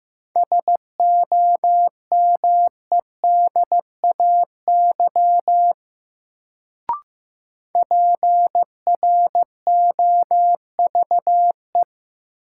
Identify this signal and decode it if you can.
Morse code, heard as a keyed tone, about 15 wpm, 700 Hz. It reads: SOMEDAY PROVE
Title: Sentences from Top 1600 Words - New Word Drill Alt - 15wpm